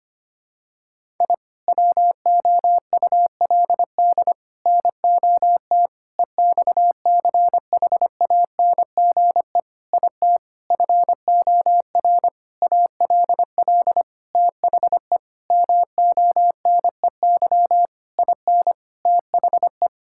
Morse code, heard as a keyed tone, about 25 wpm, 700 Hz. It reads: I WOULD NOT EXCHANGE IT FOR ALL THE MONEY IN THE
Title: Sentences from Top 1000 Words - Review Part 6 - 25wpm